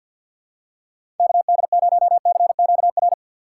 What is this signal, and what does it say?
Morse code, heard as a keyed tone, about 52 wpm, 700 Hz. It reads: KD0CQR
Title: USA Call Signs - 3x Character Spacing - 50wpm